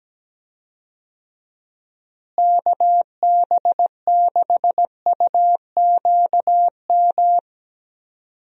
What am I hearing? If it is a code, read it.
Morse code, heard as a keyed tone, about 17 wpm, 700 Hz. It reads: KB6UQM
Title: USA Call Signs - 6 Characters - Repeated 2x - 17wpm